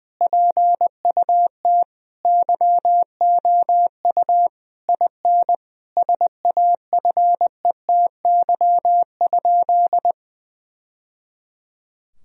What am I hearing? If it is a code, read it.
Morse code, heard as a keyed tone, about 20 wpm, 700 Hz. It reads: PUT YOU IN SAFETY?
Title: Sentences from Top 1000 Words - Review Part 4 - 20wpm